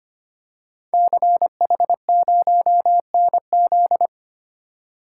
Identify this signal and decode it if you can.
Morse code, heard as a keyed tone, about 25 wpm, 700 Hz. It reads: CH0NZ
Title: Code Groups - 5 Letter:Numbers - 25wpm